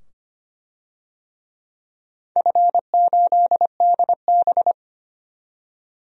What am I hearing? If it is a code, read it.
Morse code, heard as a keyed tone, about 25 wpm, 700 Hz. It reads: F8DB
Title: DXCC-100 Call Signs - 4 Characters - 25wpm